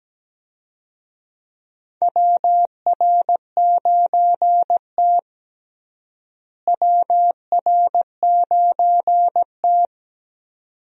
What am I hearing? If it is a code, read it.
Morse code, heard as a keyed tone, about 17 wpm, 700 Hz. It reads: WR9T WR9T
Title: USA Call Signs - 4 Characters - Repeated 2x - 17wpm